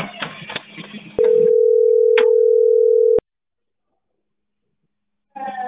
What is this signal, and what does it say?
Ringback tone (North America), 440 + 480 Hz.